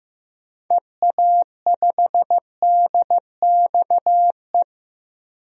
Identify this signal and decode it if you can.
Morse code, heard as a keyed tone, about 16 wpm, 700 Hz. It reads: EA5DXE
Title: DXCC-100 Call Signs - 6 Characters - 15wpm